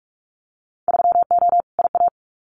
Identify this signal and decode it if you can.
Morse code, heard as a keyed tone, about 46 wpm, 700 Hz. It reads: 3O IA
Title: Winter Field Day Exchanges 45wpm